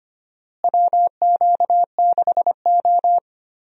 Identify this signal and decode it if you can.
Morse code, heard as a keyed tone, about 25 wpm, 700 Hz. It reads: WQ6O
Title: USA Call Signs - 4 Characters - Repeated 3x - 25wpm